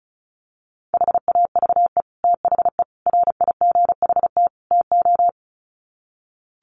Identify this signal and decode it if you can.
Morse code, heard as a keyed tone, about 34 wpm, 700 Hz. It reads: HAVE THE RIGHT TO
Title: Sets of 4 Words 35wpm